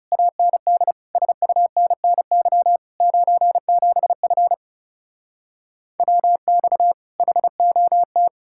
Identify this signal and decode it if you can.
Morse code, heard as a keyed tone, about 37 wpm, 700 Hz. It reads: AND SUNNY 97F WX HOT
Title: Speed-Racing Common QSO Phrases - 30wpm